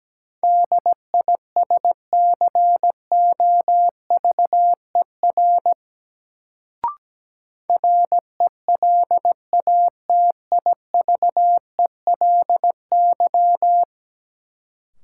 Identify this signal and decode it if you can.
Morse code, heard as a keyed tone, about 17 wpm, 700 Hz. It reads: DISCOVER RELATIVELY
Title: Sentences from Top 1100 Words - New Word Drill - 17wpm